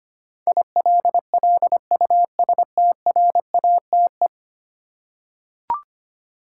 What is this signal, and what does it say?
Morse code, heard as a keyed tone, about 25 wpm, 700 Hz. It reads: ILLUSTRATE
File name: Ten Letter Words - 25wpm